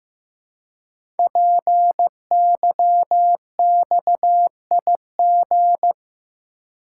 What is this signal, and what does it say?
Morse code, heard as a keyed tone, about 15 wpm, 690 Hz. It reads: PYXIG
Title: Code Groups - 5 Letters - 15wpm